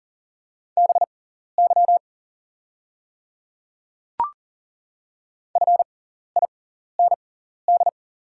Morse code, heard as a keyed tone, about 41 wpm, spaced as for 14 wpm, 690 Hz. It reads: BY FIND
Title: 6x Character Spacing - Top 200 Words - 40wpm